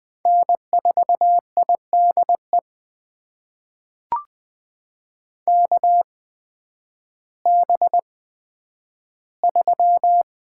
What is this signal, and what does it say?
Morse code, heard as a keyed tone, about 20 wpm, spaced as for 11 wpm, 700 Hz. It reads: N4IDE KB3
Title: USA Call Signs - 5 Characters - 8x Character Spacing - 20wpm